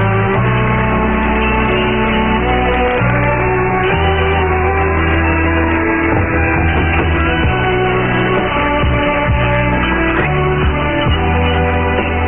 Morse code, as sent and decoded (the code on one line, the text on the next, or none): none